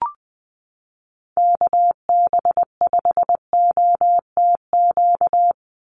Morse code, heard as a keyed tone, about 20 wpm, 700 Hz. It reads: KB5OTQ
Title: USA Call Signs 20wpm